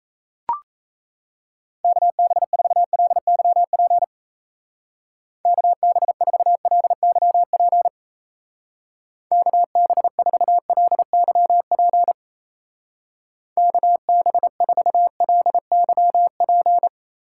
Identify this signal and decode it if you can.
Morse code, heard as a keyed tone, about 32 wpm, 700 Hz. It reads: KB4LYP KB4LYP KB4LYP KB4LYP
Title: Speed Racing - USA 2x3 Call Signs 28wpm